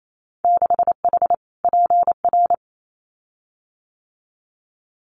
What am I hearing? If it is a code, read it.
Morse code, heard as a keyed tone, about 28 wpm, 690 Hz. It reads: 6H PR
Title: Winter Field Day Exchanges Encore 28wpm